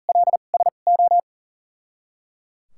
Morse code, heard as a keyed tone, about 38 wpm, spaced as for 28 wpm, 700 Hz. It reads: LSO
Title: 2x Character Spacing - Top 200 Words - 40wpm